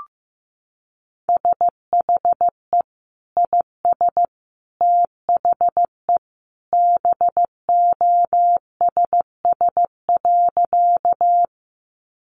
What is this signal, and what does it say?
Morse code, heard as a keyed tone, about 15 wpm, 700 Hz. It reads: SHE IS THE BOSS.
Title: Sentences from Top 700 Words 15wpm